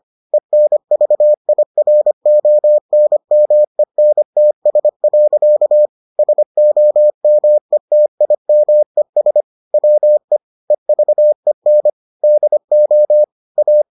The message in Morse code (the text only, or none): ENVIRONMENTS. SOMETIMES WE EVEN DO A